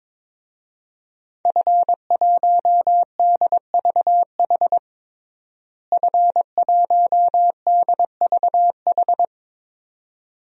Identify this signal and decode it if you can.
Morse code, heard as a keyed tone, about 22 wpm, 700 Hz. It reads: F1DVH F1DVH
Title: DXCC-100 Call Signs - 5 Characters - Repeated 2x - 22wpm